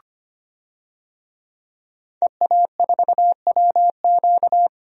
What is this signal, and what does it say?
Morse code, heard as a keyed tone, about 25 wpm, 700 Hz. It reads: EA4WQ
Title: DXCC-100 Call Signs - 5 Characters - 25wpm